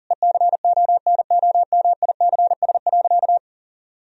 Morse code, heard as a keyed tone, about 40 wpm, 700 Hz. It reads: ECONOMICS.
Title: Sentences from Top 1000 Words - Review Part 3 - 40wpm